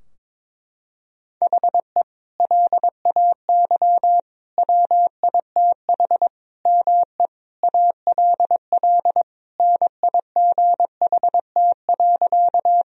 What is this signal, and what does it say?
Morse code, heard as a keyed tone, about 22 wpm, 700 Hz. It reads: HE LAY WITH ME ALL NIGHT.
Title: Sentences from Top 1000 Words - Review Part 2 - 22wpm